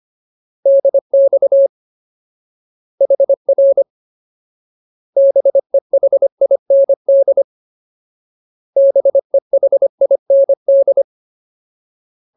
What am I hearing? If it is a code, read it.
Morse code, heard as a keyed tone, about 25 wpm, 550 Hz. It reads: DX HR BEHIND BEHIND
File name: Ultimate Word List / 25WPM Ultimate